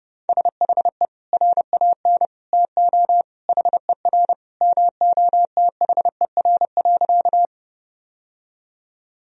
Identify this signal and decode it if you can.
Morse code, heard as a keyed tone, about 30 wpm, 700 Hz. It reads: SHE RAN TO HER MOTHER.
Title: Sentences from Top 1000 Words - Review Part 3 - 30wpm